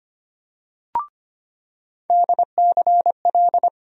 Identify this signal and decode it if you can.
Morse code, heard as a keyed tone, about 25 wpm, 700 Hz. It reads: DCL
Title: USA Call Sign Postfixes - 3 Letters 25wpm